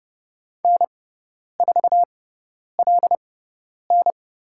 Morse code, heard as a keyed tone, about 30 wpm, spaced as for 10 wpm, 700 Hz. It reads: N4LN
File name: Lesson 17g - Letter u - Call Signs 30wpm